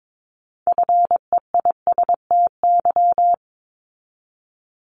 Morse code, heard as a keyed tone, about 22 wpm, 700 Hz. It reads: FEISTY
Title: Multi-letter Phonemes 22wpm